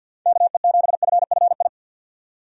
Morse code, heard as a keyed tone, about 52 wpm, 700 Hz. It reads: KE7FFI
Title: USA Call Signs - 6 Characters - 50wpm